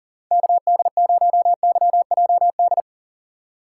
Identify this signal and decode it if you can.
Morse code, heard as a keyed tone, about 40 wpm, 700 Hz. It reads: KD0YJD
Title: USA Call Signs - 6 Characters - 6x Character Spacing - 40wpm